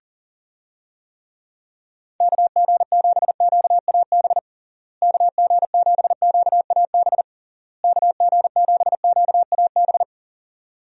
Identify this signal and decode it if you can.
Morse code, heard as a keyed tone, about 40 wpm, 700 Hz. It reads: KG7QAB KG7QAB KG7QAB
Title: USA Call Signs - 6 Characters - Repeated 3x - 40wpm